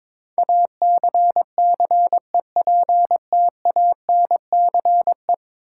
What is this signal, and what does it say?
Morse code, heard as a keyed tone, about 22 wpm, 700 Hz. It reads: ACCEPTANCE
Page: morse-code-ninja-practice-22wpm